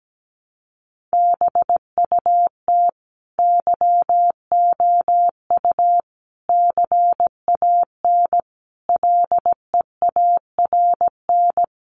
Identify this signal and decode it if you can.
Morse code, heard as a keyed tone, about 17 wpm, 700 Hz. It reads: BUT YOU CAN LEARN
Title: Sentences from Top 1500 Words 17wpm